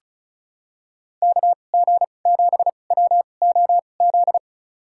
Morse code, heard as a keyed tone, about 36 wpm, spaced as for 25 wpm, 700 Hz. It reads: KG7WOZ
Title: USA Call Signs - 2x Character Spacing - 35wpm